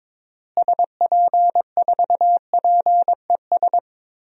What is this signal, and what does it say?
Morse code, heard as a keyed tone, about 22 wpm, 700 Hz. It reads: SP4PES
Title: DXCC-100 Call Signs - 6 Characters - Repeated 2x - 22wpm